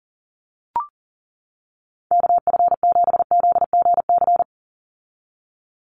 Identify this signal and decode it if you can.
Morse code, heard as a keyed tone, about 40 wpm, 700 Hz. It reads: KF7ZGC
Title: USA Call Signs 40wpm